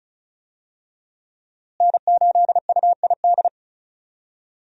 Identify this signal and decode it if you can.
Morse code, heard as a keyed tone, about 34 wpm, 700 Hz. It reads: N8UID
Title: USA Call Signs - 5 Characters - Repeated 2x - 35wpm